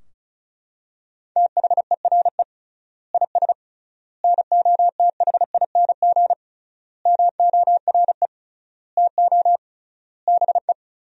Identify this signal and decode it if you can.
Morse code, heard as a keyed tone, about 34 wpm, 700 Hz. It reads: THERE IS NOTHING MORE TO BE
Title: 3x Word-Spacing - Sentences from Top 500 Words - 35wpm